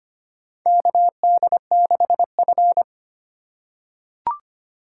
Morse code, heard as a keyed tone, about 25 wpm, 700 Hz. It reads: KD6F
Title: USA Call Signs - 3x Character Spacing - 25wpm